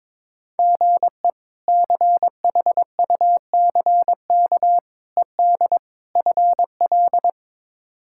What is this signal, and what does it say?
Morse code, heard as a keyed tone, about 22 wpm, 700 Hz. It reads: GE CHUCK ED FL
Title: K1USN SST QSO Practice - 22wpm